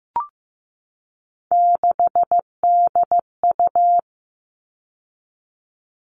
Morse code, heard as a keyed tone, about 15 wpm, 700 Hz. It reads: 6DU
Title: All Cut Numbers - 3-Digit 15wpm